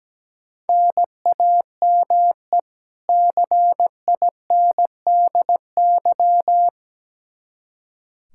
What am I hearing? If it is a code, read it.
Morse code, heard as a keyed tone, about 17 wpm, 700 Hz. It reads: NAME CINDY
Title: K1USN SST QSO Practice 17wpm